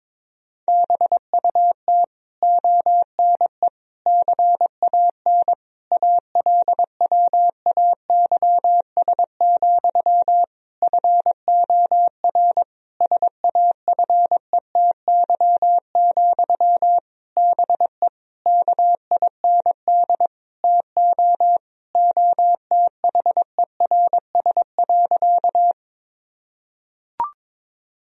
Morse code, heard as a keyed tone, about 21 wpm, 700 Hz. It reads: BUT ONE CAN ALWAYS, FOR SAFETY, BE KIND TO OTHERS.
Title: Sentences from Top 1000 Words - Review Part 4 - 22wpm